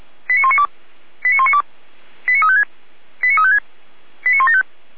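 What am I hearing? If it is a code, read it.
Five-tone selective call (ZVEI), heard as quick runs of five tones, tones of about 80 ms. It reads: 89192 89192 89356 89356 89175
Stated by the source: ZVEI five-tone call received on SDR.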